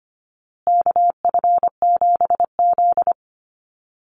Morse code, heard as a keyed tone, about 25 wpm, 700 Hz. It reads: KF7Z
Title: Call Signs Repeated 3x - 25wpm